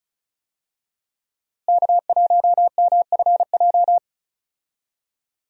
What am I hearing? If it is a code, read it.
Morse code, heard as a keyed tone, about 36 wpm, 700 Hz. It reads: K1MFJ
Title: USA Call Signs - 5 Characters - Repeated 3x - 35wpm